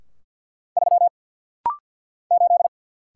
Morse code, heard as a keyed tone, about 50 wpm, 700 Hz. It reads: 3 8
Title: Single Letter-Number - Rapid-Fire 50wpm